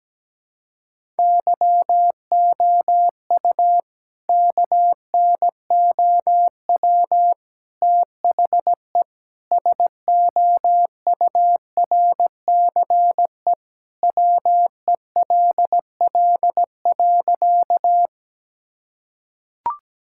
Morse code, heard as a keyed tone, about 17 wpm, 700 Hz. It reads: YOU KNOW THE SOURCE WELL.
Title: Sentences from Top 1000 Words - Review Part 5 - 17wpm